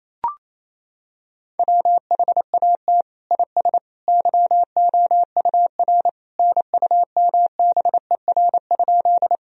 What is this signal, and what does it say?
Morse code, heard as a keyed tone, about 28 wpm, 700 Hz. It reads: WHAT IS YOUR NUMBER?
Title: Sentences from Top 1000 Words - Review Part 1 - 28wpm